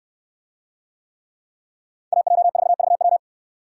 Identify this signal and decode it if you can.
Morse code, heard as a keyed tone, about 69 wpm, 700 Hz. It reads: AJ4VW